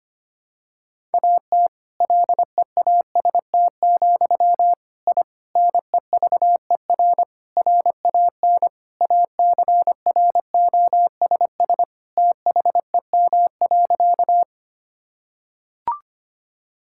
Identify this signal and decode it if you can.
Morse code, heard as a keyed tone, about 25 wpm, 700 Hz. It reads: AT LEAST, I NEVER RAN ACROSS THEM.
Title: Sentences from Top 1000 Words - Review Part 5 - 25wpm